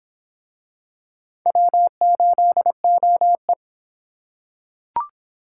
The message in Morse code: W8OE

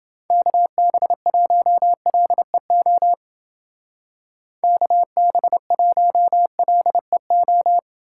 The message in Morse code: KB1LEO KB1LEO

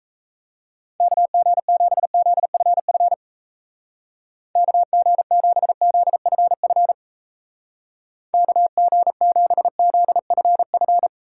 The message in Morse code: KG7ZFF KG7ZFF KG7ZFF

